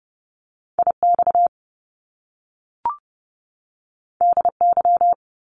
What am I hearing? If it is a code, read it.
Morse code, heard as a keyed tone, about 30 wpm, 700 Hz. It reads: IX DY
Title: Top Two Letter Combinations in Words 30wpm